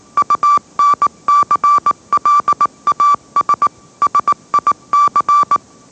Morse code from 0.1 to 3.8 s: UNCLAS